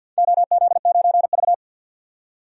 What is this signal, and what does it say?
Morse code, heard as a keyed tone, about 51 wpm, 700 Hz. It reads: OZ9V
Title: DXCC-100 Call Signs - 4 Characters - 50wpm